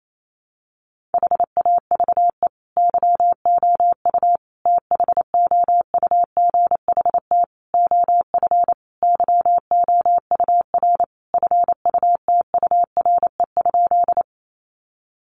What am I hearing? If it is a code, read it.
Morse code, heard as a keyed tone, about 28 wpm, 700 Hz. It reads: HAVE YOU THOUGHT OF YOUR FUTURE?
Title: Sentences from Top 700 Words 28wpm